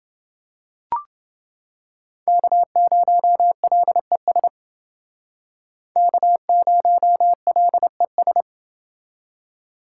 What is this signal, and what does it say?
Morse code, heard as a keyed tone, about 29 wpm, 700 Hz. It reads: K0LES K0LES